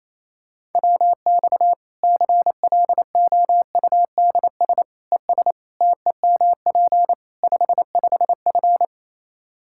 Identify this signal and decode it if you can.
Morse code, heard as a keyed tone, about 28 wpm, 700 Hz. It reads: WX CLOUDS ES TEMP 55F
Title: K1USN SST QSO Practice 28wpm